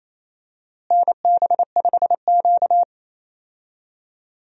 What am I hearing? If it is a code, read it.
Morse code, heard as a keyed tone, about 28 wpm, 700 Hz. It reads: NB5Q
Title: USA Call Signs - 4 Characters - Repeated 3x - 28wpm